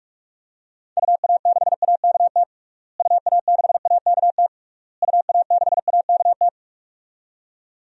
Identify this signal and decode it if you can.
Morse code, heard as a keyed tone, about 46 wpm, 700 Hz. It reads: UA6AKT UA6AKT UA6AKT
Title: DXCC-100 Call Signs - 6 Characters - Repeated 3x - 45wpm